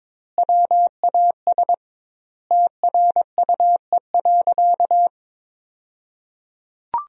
Morse code, heard as a keyed tone, about 22 wpm, 700 Hz. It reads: WAS TRUE.
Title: 2x Word-Spacing - Sentences from Top 400 Words - 22wpm